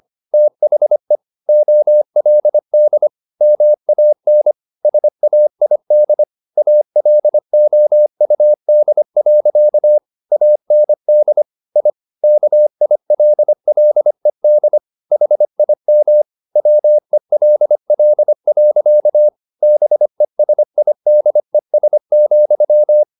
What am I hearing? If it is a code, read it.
Morse code, heard as a keyed tone, about 25 wpm, 600 Hz. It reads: THE OLD MAN SAID ALOUD. AND I KILLED HIM WELL. BESIDES,